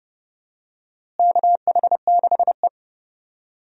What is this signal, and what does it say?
Morse code, heard as a keyed tone, about 31 wpm, 700 Hz. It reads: KH6E
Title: DXCC-100 Call Signs - 4 Characters - Repeated 2x - 30wpm